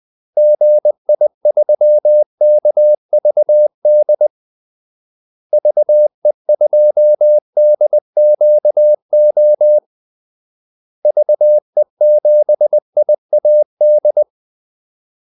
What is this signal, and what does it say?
Morse code, heard as a keyed tone, about 20 wpm, 600 Hz. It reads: GI3KVD VE2DQO VE7IAD
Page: Kaldesignaler 20 wpm